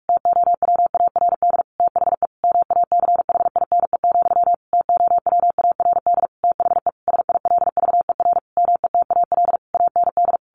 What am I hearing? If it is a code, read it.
Morse code, heard as a keyed tone, about 45 wpm, 700 Hz. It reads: TOWARD THE MACHINE, TOWARD THE SILVER METAL AND
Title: A Sound Of Thunder 45wpm